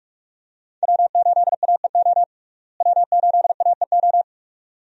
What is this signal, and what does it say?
Morse code, heard as a keyed tone, about 46 wpm, 700 Hz. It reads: W8AEO W8AEO
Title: USA Call Signs - 5 Characters - Repeated 2x - 45wpm